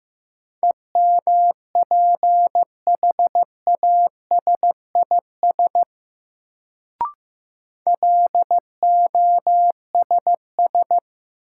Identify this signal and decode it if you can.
Morse code, heard as a keyed tone, about 15 wpm, 700 Hz. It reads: EMPHASIS LOSS
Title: Sentences from Top 1400 Words - New Word Drill - 15wpm